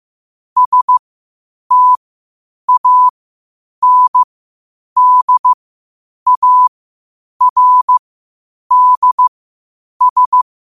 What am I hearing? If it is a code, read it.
Morse code, heard as a keyed tone, about 15 wpm, spaced as for 8 wpm, 1000 Hz. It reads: STANDARDS
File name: qotdfiles / 8wpm